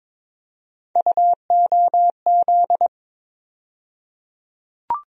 Morse code, heard as a keyed tone, about 22 wpm, 700 Hz. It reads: UOZ
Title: USA Call Sign Postfixes - 3 Letters 22wpm